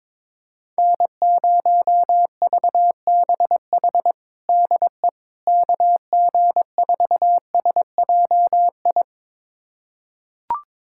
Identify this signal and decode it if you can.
Morse code, heard as a keyed tone, about 22 wpm, 700 Hz. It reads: N0VBH DE KG4SJI
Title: USA Callsign DE Callsign 22wpm